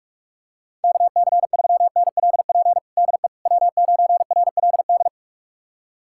Morse code, heard as a keyed tone, about 46 wpm, 690 Hz. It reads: KC3NLP DE W9RLD
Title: USA Callsign DE Callsign 45wpm